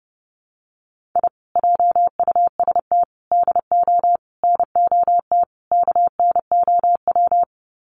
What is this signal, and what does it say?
Morse code, heard as a keyed tone, about 30 wpm, 700 Hz. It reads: I JUST DO NOT KNOW
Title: Sets of 5 Words Encore 30wpm